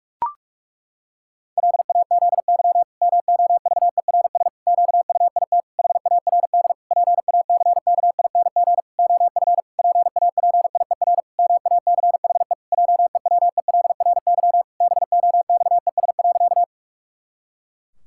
Morse code, heard as a keyed tone, about 41 wpm, 700 Hz. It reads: LAZY MOVERS QUIT HARD PACKING OF PAPIER MACHE JEWELRY BOXES.
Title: Pangram Challenge v2 - 45wpm